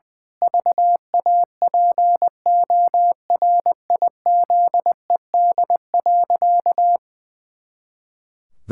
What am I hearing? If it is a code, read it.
Morse code, heard as a keyed tone, about 20 wpm, 700 Hz. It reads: VAPORIZED.
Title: Pangram Challenge v2 - 20wpm